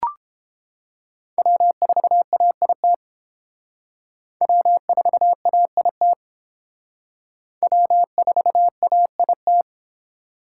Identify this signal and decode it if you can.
Morse code, heard as a keyed tone, about 29 wpm, 700 Hz. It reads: W4AIT W4AIT W4AIT